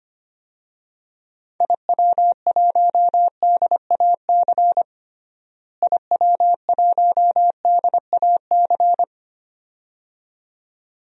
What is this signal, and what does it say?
Morse code, heard as a keyed tone, about 25 wpm, 700 Hz. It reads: IW1DAC IW1DAC
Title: DXCC-100 Call Signs - 6 Characters - Repeated 2x - 25wpm